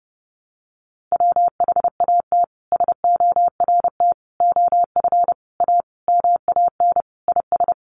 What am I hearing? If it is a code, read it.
Morse code, heard as a keyed tone, about 30 wpm, 700 Hz. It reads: WHAT SORT OF A MAN IS
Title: Sentences from Top 1300 Words 30wpm